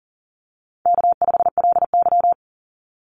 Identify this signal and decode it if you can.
Morse code, heard as a keyed tone, about 40 wpm, 700 Hz. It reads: K5LY
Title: Call Signs Repeated 2x - 40wpm